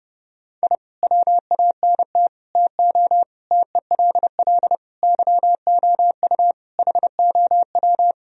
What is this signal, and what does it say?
Morse code, heard as a keyed tone, about 32 wpm, 700 Hz. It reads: I WANT TO TELL YOU HOW
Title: Sentences from Top 1000 Words - Review Part 4 - 30wpm